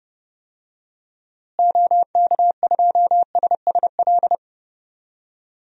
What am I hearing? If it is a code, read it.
Morse code, heard as a keyed tone, about 32 wpm, 700 Hz. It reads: OK2SSL
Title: DXCC-100 Call Signs - 6 Characters - Repeated 2x - 30wpm